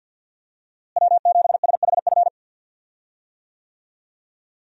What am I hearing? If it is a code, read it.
Morse code, heard as a keyed tone, about 52 wpm, 700 Hz. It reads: W7SHF
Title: USA Call Signs - 5 Characters - Repeated 2x - 50wpm